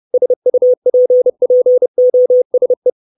Morse code, heard as a keyed tone, about 30 wpm, 500 Hz. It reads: SUPPOSE
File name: 632 Words 30WPM